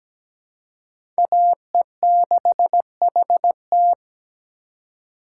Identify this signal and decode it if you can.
Morse code, heard as a keyed tone, about 17 wpm, 700 Hz. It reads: AE6HT
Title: USA Call Signs - 5 Characters - 3x Character Spacing - 17wpm